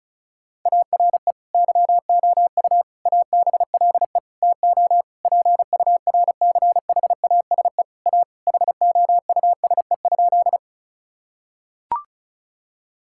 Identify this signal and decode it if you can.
Morse code, heard as a keyed tone, about 36 wpm, 700 Hz. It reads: ARE YOU ABLE TO PURCHASE A HOUSE?
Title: Sentences from Top 1000 Words - Review Part 5 - 35wpm